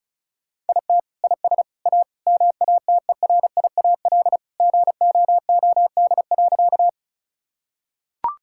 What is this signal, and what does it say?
Morse code, heard as a keyed tone, about 36 wpm, 700 Hz. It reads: IT IS A MATERIAL GOOD.
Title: Sentences from Top 1000 Words - Review Part 5 - 35wpm